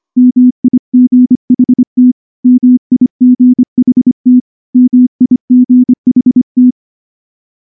Morse code, heard as a keyed tone, about 24 wpm, 250 Hz. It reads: MIGHT MIGHT MIGHT
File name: morse-training-25WPM